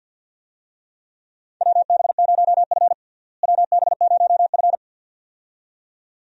Morse code, heard as a keyed tone, about 51 wpm, 700 Hz. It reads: WB0F WB0F
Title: USA Call Signs - 4 Characters - Repeated 2x - 50wpm